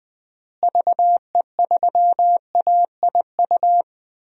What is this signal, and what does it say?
Morse code, heard as a keyed tone, about 20 wpm, 700 Hz. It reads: VE3AIU
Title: DXCC-100 Call Signs - 6 Characters - Repeated 2x - 20wpm